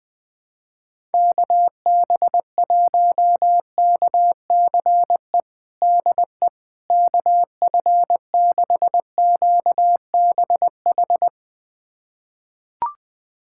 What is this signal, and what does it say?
Morse code, heard as a keyed tone, about 20 wpm, 700 Hz. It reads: KB1KCE DE KF6QBH
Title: USA Callsign DE Callsign 20wpm